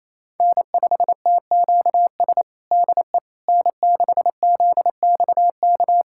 Morse code, heard as a keyed tone, about 28 wpm, 700 Hz. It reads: N5TQS DE N6ZXK
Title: USA Callsign DE Callsign - 5 Characters - 28wpm